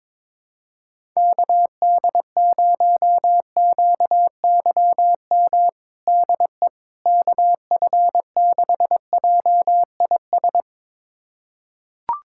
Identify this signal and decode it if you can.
Morse code, heard as a keyed tone, about 22 wpm, 700 Hz. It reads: KD0QYM DE KF6JIS
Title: USA Callsign DE Callsign - 6 Characters - 22wpm